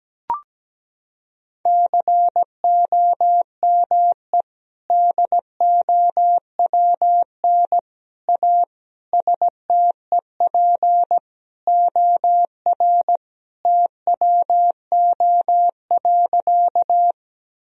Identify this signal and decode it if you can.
Morse code, heard as a keyed tone, about 17 wpm, 700 Hz. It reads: COME DOWN A STEP OR TWO.
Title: Sentences from Top 1000 Words - Review Part 2 - 17wpm